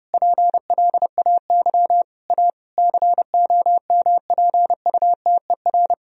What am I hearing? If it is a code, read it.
Morse code, heard as a keyed tone, about 30 wpm, 700 Hz. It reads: PLAY A COMPUTER
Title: Sentences from Top 1000 Words - Review Part 3 - 30wpm